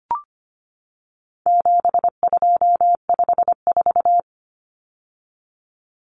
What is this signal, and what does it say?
Morse code, heard as a keyed tone, about 25 wpm, 700 Hz. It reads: 7254
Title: Numbers - 4-Digit 25wpm